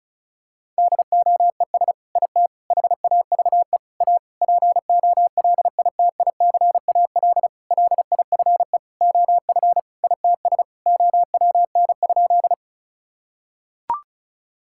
Morse code, heard as a keyed tone, about 34 wpm, 690 Hz. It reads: DOES IT HAVE A POLITICAL LIFE OF ITS OWN?
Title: Sentences from Top 1000 Words - Review Part 5 - 35wpm